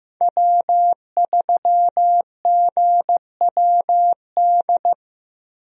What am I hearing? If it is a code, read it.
Morse code, heard as a keyed tone, about 15 wpm, 700 Hz. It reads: W3GWD
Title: USA Call Signs - 5 Characters - Repeated 2x - 15wpm